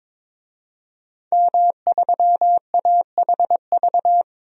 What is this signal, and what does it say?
Morse code, heard as a keyed tone, about 22 wpm, 700 Hz. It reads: M3AHV
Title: DXCC-100 Call Signs - 5 Characters - Repeated 2x - 22wpm